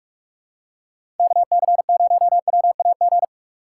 Morse code, heard as a keyed tone, about 44 wpm, 700 Hz. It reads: KC0WAG
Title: USA Call Signs - 6x Character Spacing - 45wpm